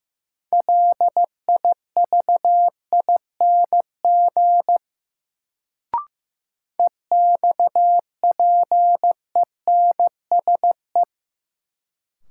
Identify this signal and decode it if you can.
Morse code, heard as a keyed tone, about 15 wpm, 700 Hz. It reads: LIVING EXPENSE
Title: Sets of 4 Words - New Word Drill - 15wpm